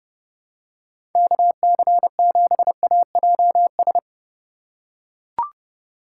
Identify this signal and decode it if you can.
Morse code, heard as a keyed tone, about 31 wpm, 700 Hz. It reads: KC7AJS
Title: USA Call Signs - 6 Characters - 2x Character Spacing - 30wpm